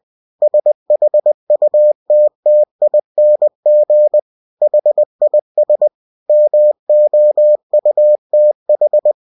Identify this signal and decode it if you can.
Morse code, heard as a keyed tone, about 20 wpm, 600 Hz. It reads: SHUTTING HIS MOUTH
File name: cw-books / winnie-the-pooh / winnie-20wpm-page-0064